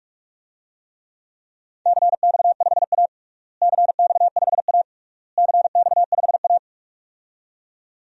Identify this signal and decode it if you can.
Morse code, heard as a keyed tone, about 42 wpm, 700 Hz. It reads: CX5A CX5A CX5A
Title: DXCC-100 Call Signs - 4 Characters - Repeated 3x - 45wpm